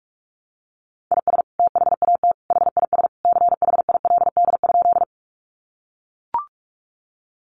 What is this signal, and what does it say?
Morse code, heard as a keyed tone, about 44 wpm, 700 Hz. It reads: IS THAT HIS CHILD?
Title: Sentences from Top 800 Words 45wpm